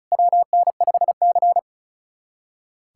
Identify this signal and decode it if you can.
Morse code, heard as a keyed tone, about 36 wpm, 700 Hz. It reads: WN5C
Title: USA Callsign DE Callsign - 4 Characters - 35wpm